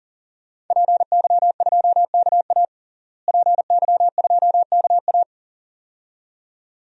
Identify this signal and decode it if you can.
Morse code, heard as a keyed tone, about 37 wpm, 700 Hz. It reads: PY2KA PY2KA
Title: DXCC-100 Call Signs - 5 Characters - Repeated 2x - 40wpm